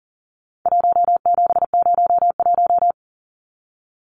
Morse code, heard as a keyed tone, about 40 wpm, 700 Hz. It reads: 1701
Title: Numbers - 4-Digit 40wpm